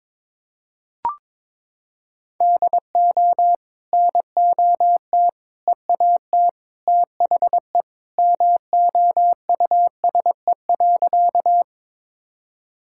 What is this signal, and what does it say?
Morse code, heard as a keyed tone, about 22 wpm, 700 Hz. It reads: DO NOT EAT THE MOUSE.
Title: Sentences from Top 1000 Words - Review Part 5 - 22wpm